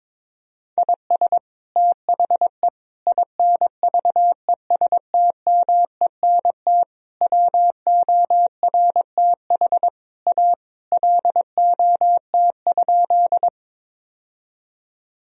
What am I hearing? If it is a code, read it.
Morse code, heard as a keyed tone, about 22 wpm, 700 Hz. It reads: IS THE INVESTMENT WORTH A LOT?
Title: Sentences from Top 1000 Words - Review Part 4 - 22wpm